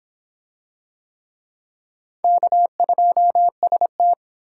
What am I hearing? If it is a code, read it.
Morse code, heard as a keyed tone, about 22 wpm, 700 Hz. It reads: K2ST